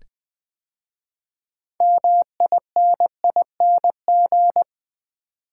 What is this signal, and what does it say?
Morse code, heard as a keyed tone, about 20 wpm, 700 Hz. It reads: MINING
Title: 6 Letter Words - First 3 Letters Given - 20wpm